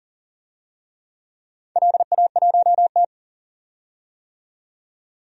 Morse code, heard as a keyed tone, about 38 wpm, 700 Hz. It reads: LA1T
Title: DXCC-100 Call Signs - 4 Characters - 40wpm